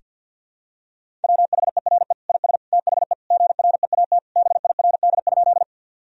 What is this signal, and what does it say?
Morse code, heard as a keyed tone, about 50 wpm, 700 Hz. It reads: WHERE IS THE GREAT BIRD?
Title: Sentences from Top 1000 Words - Review Part 2 - 50wpm